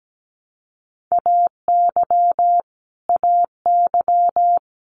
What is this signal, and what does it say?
Morse code, heard as a keyed tone, about 17 wpm, 700 Hz. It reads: AY AY
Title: Multi-letter Phonemes Repeated 3x 17wpm